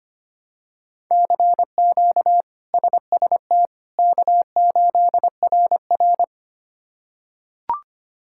Morse code, heard as a keyed tone, about 25 wpm, 700 Hz. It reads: CQ SST K8RR
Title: K1USN SST QSO Practice - 25wpm